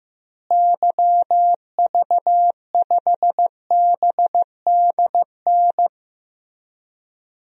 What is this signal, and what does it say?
Morse code, heard as a keyed tone, about 15 wpm, 700 Hz. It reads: YV5BDN
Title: DXCC-100 Call Signs - 6 Characters - Repeated 2x - 15wpm